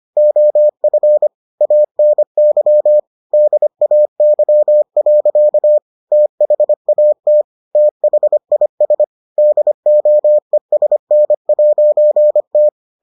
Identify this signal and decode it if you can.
Morse code, heard as a keyed tone, about 25 wpm, 600 Hz. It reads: OF ANY DAY. THAT THIS DOESN'T